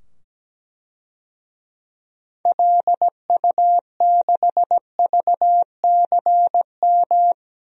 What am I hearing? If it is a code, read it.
Morse code, heard as a keyed tone, about 17 wpm, 700 Hz. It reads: LU6VCM
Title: DXCC-100 Call Signs - 6 Characters - 17wpm